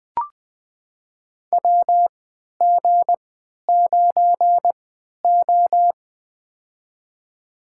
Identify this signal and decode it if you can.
Morse code, heard as a keyed tone, about 20 wpm, spaced as for 11 wpm, 700 Hz. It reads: WG9O
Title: USA Call Signs - 3x Character Spacing - 20wpm